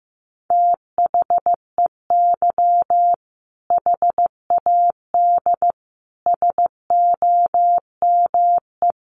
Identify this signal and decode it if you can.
Morse code, heard as a keyed tone, about 15 wpm, 700 Hz. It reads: THEY HAD SOME
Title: Sentences from Top 1500 Words 15wpm